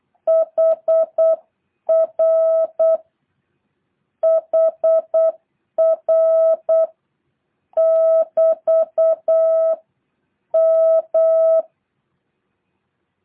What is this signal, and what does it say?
Morse code, heard as a keyed tone, about 8 wpm, 650 Hz. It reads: HR HR <BT> M
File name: jh8cbh